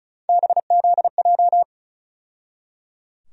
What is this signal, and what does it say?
Morse code, heard as a keyed tone, about 36 wpm, 700 Hz. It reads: BZJ
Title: USA Call Sign Postfixes - 3 Letters 35wpm